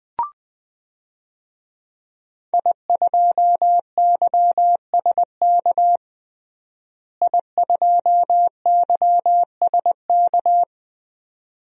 Morse code, heard as a keyed tone, about 21 wpm, 700 Hz. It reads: I2YSK I2YSK
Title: DXCC-100 Call Signs - 5 Characters - Repeated 2x - 20wpm